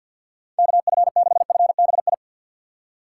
Morse code, heard as a keyed tone, about 45 wpm, 700 Hz. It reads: KF6FBI
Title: USA Call Signs - 6 Characters - 6x Character Spacing - 50wpm